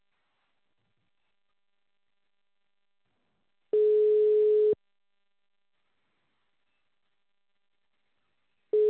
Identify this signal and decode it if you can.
Ringback tone (Europe, 425 Hz).